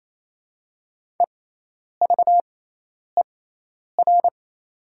Morse code, heard as a keyed tone, about 28 wpm, spaced as for 10 wpm, 700 Hz. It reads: EVER
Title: 6x Character Spacing - Top 500 Words - 28wpm